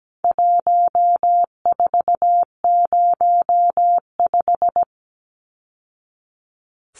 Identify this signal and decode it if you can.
Morse code, heard as a keyed tone, about 17 wpm, 700 Hz. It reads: 1405
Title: Numbers - 4-Digit 17wpm